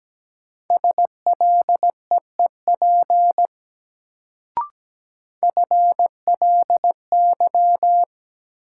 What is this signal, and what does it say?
Morse code, heard as a keyed tone, about 17 wpm, 700 Hz. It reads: SLEEP FLY
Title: Sentences from Top 500 Words - New Word Drill - 17wpm